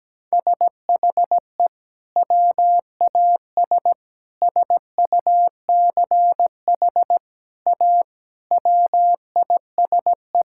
Morse code, heard as a keyed tone, about 17 wpm, 700 Hz. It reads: SHE WAS SUCH A WISE
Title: Sentences from Top 1600 Words 17wpm